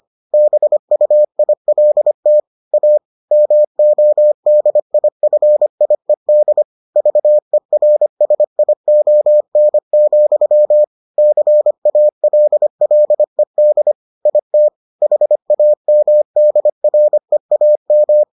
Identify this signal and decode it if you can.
Morse code, heard as a keyed tone, about 25 wpm, 600 Hz. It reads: BUILT A MODIFIED VERSION, CALLED IT HAMDREAM